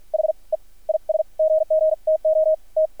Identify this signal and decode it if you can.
Morse code, heard as a keyed tone, about 53 wpm, spaced as for 40 wpm, 630 Hz. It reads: HE IS GG TO T
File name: archived / CRSnet